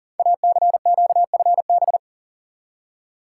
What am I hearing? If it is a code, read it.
Morse code, heard as a keyed tone, about 42 wpm, 700 Hz. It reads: ACQFB
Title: Code Groups - 5 Letters - 40wpm